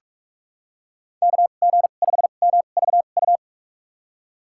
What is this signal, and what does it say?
Morse code, heard as a keyed tone, about 43 wpm, spaced as for 32 wpm, 700 Hz. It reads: KG5MVU
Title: USA Call Signs - 6 Characters - 2x Character Spacing - 45wpm